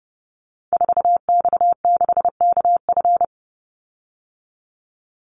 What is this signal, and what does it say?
Morse code, heard as a keyed tone, about 30 wpm, 700 Hz. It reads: 4X6KF
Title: Call Signs Repeated 2x - 30wpm